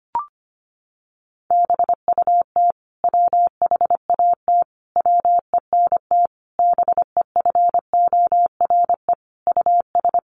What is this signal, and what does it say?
Morse code, heard as a keyed tone, about 25 wpm, 700 Hz. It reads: BUT WHAT WENT BEFORE US
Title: Sentences from Top 1400 Words 25wpm